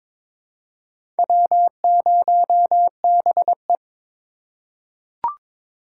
Morse code, heard as a keyed tone, about 22 wpm, 690 Hz. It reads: W0BE
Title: USA Call Signs - 4 Characters - 3x Character Spacing - 22wpm